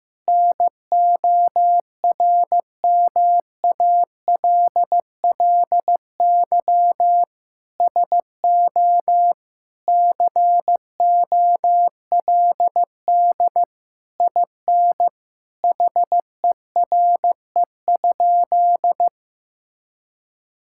Morse code, heard as a keyed tone, about 15 wpm, 700 Hz. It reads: NORMALLY SO COLD IN HERE?
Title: Sentences from Top 1000 Words - Review Part 6 - 15wpm